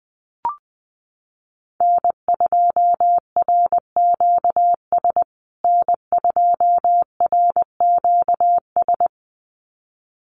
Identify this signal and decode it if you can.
Morse code, heard as a keyed tone, about 20 wpm, 700 Hz. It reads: N2RQS N2RQS
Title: USA Call Signs Repeated 2x 20wpm